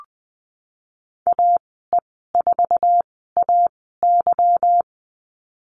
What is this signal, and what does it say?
Morse code, heard as a keyed tone, about 20 wpm, spaced as for 15 wpm, 700 Hz. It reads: AE4AY
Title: 154 Lesson 25g - Letter p - Call Signs 20wpm